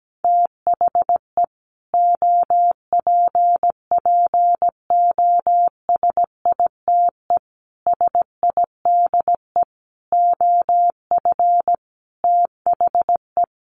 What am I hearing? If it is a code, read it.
Morse code, heard as a keyed tone, about 17 wpm, 700 Hz. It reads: THE OPPOSITE SIDE OF THE
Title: Sets of 5 Words Encore 17wpm